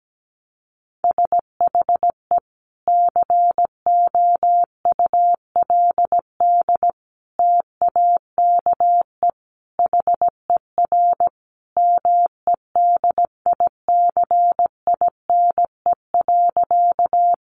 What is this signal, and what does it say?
Morse code, heard as a keyed tone, about 17 wpm, 700 Hz. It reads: SHE COULD TAKE HER MEDICINE.
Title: Sentences from Top 900 Words 17wpm